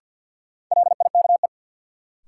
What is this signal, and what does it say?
Morse code, heard as a keyed tone, about 48 wpm, 700 Hz. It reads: LIKE
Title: Sentences from Top 100 Words - New Word Drill - 50wpm